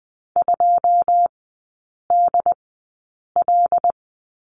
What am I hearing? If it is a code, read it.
Morse code, heard as a keyed tone, about 20 wpm, spaced as for 8 wpm, 700 Hz. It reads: 2DL
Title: 093 Lesson 16e - Letter c - Three Characters 20wpm